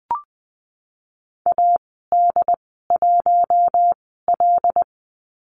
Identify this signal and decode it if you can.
Morse code, heard as a keyed tone, about 20 wpm, spaced as for 15 wpm, 700 Hz. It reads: AD1L
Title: 082 Lesson 14g - Number 2 - Call Signs 20wpm